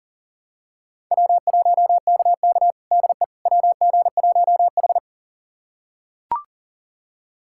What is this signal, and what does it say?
Morse code, heard as a keyed tone, about 40 wpm, 700 Hz. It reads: W1KK DE WG1H
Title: USA Callsign DE Callsign - 4 Characters - 40wpm